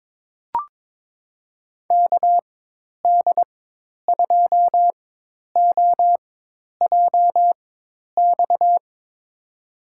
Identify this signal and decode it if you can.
Morse code, heard as a keyed tone, about 22 wpm, spaced as for 10 wpm, 700 Hz. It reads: KD2OJX
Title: USA Call Signs - 6 Characters - 4x Character Spacing - 22wpm